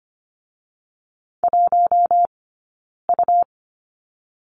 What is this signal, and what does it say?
Morse code, heard as a keyed tone, about 25 wpm, spaced as for 9 wpm, 700 Hz. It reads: 1U
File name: course / 25:10wpm / 099 Lesson 17d - Letter u - Two Characters 25wpm